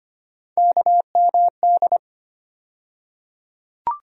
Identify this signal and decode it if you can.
Morse code, heard as a keyed tone, about 25 wpm, 700 Hz. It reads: KMD
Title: USA Call Sign Postfixes - 3 Letters 25wpm